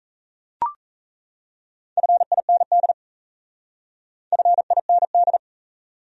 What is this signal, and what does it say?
Morse code, heard as a keyed tone, about 35 wpm, 700 Hz. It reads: FIND FIND